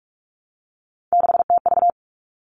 Morse code, heard as a keyed tone, about 44 wpm, 700 Hz. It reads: 6TV
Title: All Cut Numbers - 3-Digit 45wpm